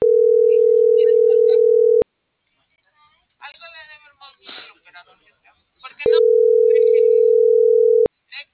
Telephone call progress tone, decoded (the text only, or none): ringback tone